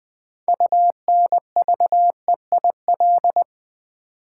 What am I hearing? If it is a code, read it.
Morse code, heard as a keyed tone, about 20 wpm, 700 Hz. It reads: UNVEIL
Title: 6 Letter Words - First 3 Letters Given - 20wpm